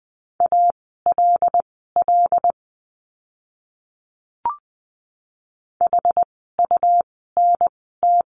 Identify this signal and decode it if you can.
Morse code, heard as a keyed tone, about 20 wpm, spaced as for 15 wpm, 700 Hz. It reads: ALL HUNT
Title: 101 Lesson 17f - Letter u - Words 20wpm